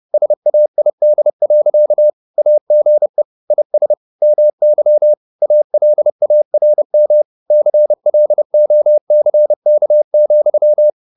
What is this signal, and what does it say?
Morse code, heard as a keyed tone, about 30 wpm, 600 Hz. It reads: SAID. AGE IS MY ALARM CLOCK,
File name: the old man and the sea / the old man and the sea 1 30wpm